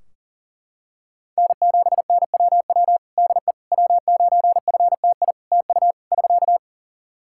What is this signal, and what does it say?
Morse code, heard as a keyed tone, about 38 wpm, 700 Hz. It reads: N7NWW DE W9FTI TU <SK>
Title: Practice QSO - Exchange 4 - Ending - 40wpm